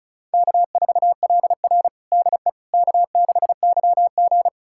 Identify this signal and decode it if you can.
Morse code, heard as a keyed tone, about 36 wpm, 700 Hz. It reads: K4LR DE K6YG
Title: USA Callsign DE Callsign - 4 Characters - 35wpm